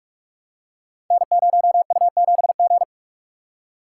Morse code, heard as a keyed tone, about 44 wpm, 700 Hz. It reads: N0U7G
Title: Code Groups - 5 Letter:Numbers - 45wpm